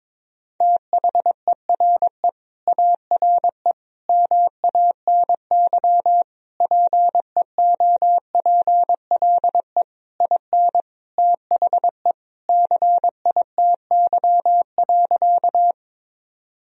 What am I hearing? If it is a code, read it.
Morse code, heard as a keyed tone, about 22 wpm, 700 Hz. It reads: THERE ARE MANY PEOPLE IN THE CITY.
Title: Sentences from Top 1000 Words - Review Part 1 - 22wpm